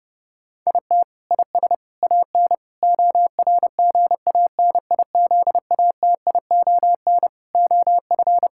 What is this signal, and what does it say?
Morse code, heard as a keyed tone, about 32 wpm, 700 Hz. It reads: IT IS AN ORGANIZATION OF
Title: Sentences from Top 1000 Words - Review Part 6 - 30wpm